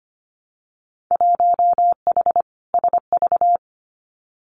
Morse code, heard as a keyed tone, about 25 wpm, 700 Hz. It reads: 1H SV
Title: Winter Field Day Exchanges 25wpm